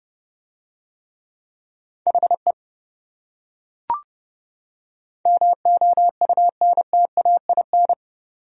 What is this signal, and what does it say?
Morse code, heard as a keyed tone, about 28 wpm, 700 Hz. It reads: HE MOUNTAIN